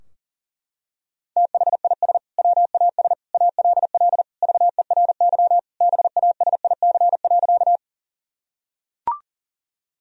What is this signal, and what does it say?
Morse code, heard as a keyed tone, about 40 wpm, 700 Hz. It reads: THIS WAS ALL VERY BASIC.
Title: Sentences from Top 1000 Words - Review Part 3 - 40wpm